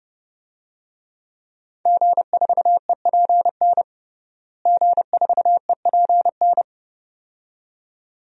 Morse code, heard as a keyed tone, about 30 wpm, 700 Hz. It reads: G4EPN G4EPN
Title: DXCC-100 Call Signs - 5 Characters - Repeated 2x - 30wpm